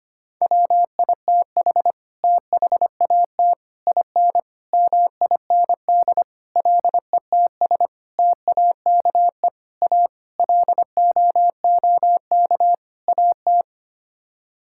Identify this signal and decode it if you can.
Morse code, heard as a keyed tone, about 25 wpm, 700 Hz. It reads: WITH THAT IN MIND LETS TAKE A LOOK AT
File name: Sets of 9 Words - 25wpm